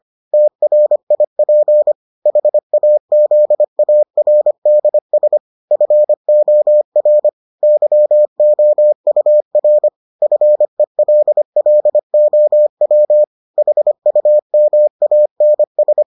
Morse code, heard as a keyed tone, about 25 wpm, 600 Hz. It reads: TRIP HAZARDS FOR YOUR FELLOW HUMANS